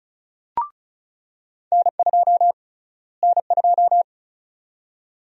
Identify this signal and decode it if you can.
Morse code, heard as a keyed tone, about 34 wpm, 700 Hz. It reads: N2 N2
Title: USA Call Sign Prefixes - Repeated 2x - 35wpm